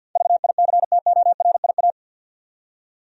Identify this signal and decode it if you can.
Morse code, heard as a keyed tone, about 48 wpm, 700 Hz. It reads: VICTORIA
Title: USA Common Names - 50wpm